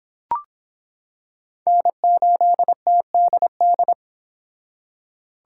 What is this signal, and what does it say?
Morse code, heard as a keyed tone, about 26 wpm, 700 Hz. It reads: N8TDD